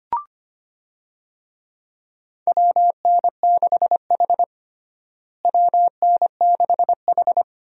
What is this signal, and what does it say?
Morse code, heard as a keyed tone, about 25 wpm, 700 Hz. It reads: WN6H WN6H
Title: USA Call Signs - 4 Characters - Repeated 2x - 25wpm